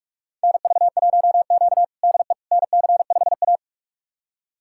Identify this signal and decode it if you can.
Morse code, heard as a keyed tone, about 47 wpm, 700 Hz. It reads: NV1Q DE NC5A
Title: USA Callsign DE Callsign - 4 Characters - 45wpm